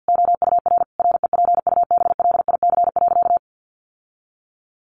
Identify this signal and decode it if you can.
Morse code, heard as a keyed tone, about 50 wpm, 700 Hz. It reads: OUR REPUBLIC.
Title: Sentences from Top 1500 Words 50wpm